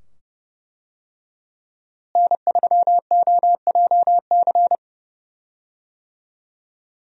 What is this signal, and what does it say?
Morse code, heard as a keyed tone, about 30 wpm, 700 Hz. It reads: N3OJC
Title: USA Call Signs - 5 Characters - Repeated 2x - 30wpm